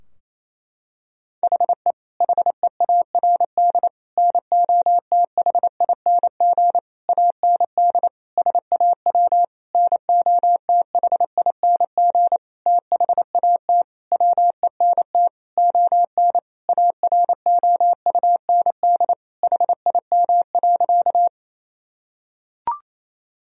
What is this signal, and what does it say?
Morse code, heard as a keyed tone, about 28 wpm, 700 Hz. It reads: HE HEARD NOTHING AND SAW NOTHING THAT WENT ON AROUND HIM.
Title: Sentences from Top 1000 Words - Review Part 3 - 28wpm